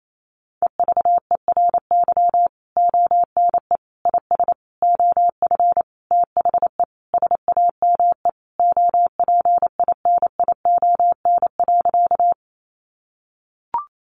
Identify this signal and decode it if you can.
Morse code, heard as a keyed tone, about 28 wpm, 700 Hz. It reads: EVERY ONE IS OF THE SAME OPINION.
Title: Sentences from Top 1200 Words 28wpm